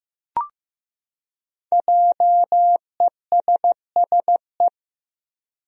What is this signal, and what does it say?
Morse code, heard as a keyed tone, about 15 wpm, 700 Hz. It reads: JESSE
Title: USA Common Names - 15wpm